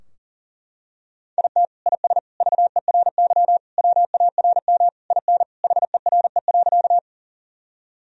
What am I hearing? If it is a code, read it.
Morse code, heard as a keyed tone, about 40 wpm, 700 Hz. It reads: IT IS VERY WARM IN HERE.
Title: Sentences from Top 1000 Words - Review Part 2 - 40wpm